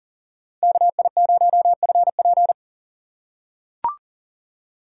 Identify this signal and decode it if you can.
Morse code, heard as a keyed tone, about 40 wpm, 700 Hz. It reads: KI0FP
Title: USA Call Signs - 3x Character Spacing - 40wpm